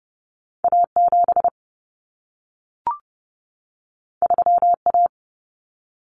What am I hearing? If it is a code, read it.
Morse code, heard as a keyed tone, about 30 wpm, 700 Hz. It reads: A7 3A
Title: Common Cut Numbers - 2-Digit 30wpm